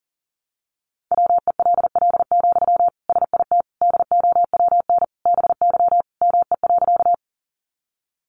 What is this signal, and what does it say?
Morse code, heard as a keyed tone, about 40 wpm, 700 Hz. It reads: WELL, SIT DOWN BY ME.
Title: Sentences from Top 500 Words 40wpm